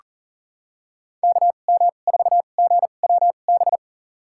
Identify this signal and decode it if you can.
Morse code, heard as a keyed tone, about 40 wpm, spaced as for 29 wpm, 700 Hz. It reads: KM4GWB
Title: USA Call Signs - 6 Characters - 2x Character Spacing - 40wpm